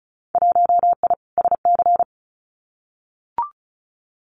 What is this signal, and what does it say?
Morse code, heard as a keyed tone, about 36 wpm, 700 Hz. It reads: 1I SC
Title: Winter Field Day Exchanges 35wpm